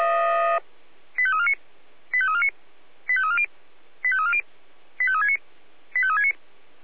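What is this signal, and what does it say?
Five-tone selective call (ZVEI), heard as quick runs of five tones, tones of about 70 ms. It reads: 85349 85349 85340 85340 85369 85369
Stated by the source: Received on an SDR.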